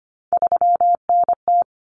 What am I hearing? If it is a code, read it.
Morse code, heard as a keyed tone, about 25 wpm, 700 Hz. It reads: 3NT
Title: Common Cut Numbers - 3-Digit 25wpm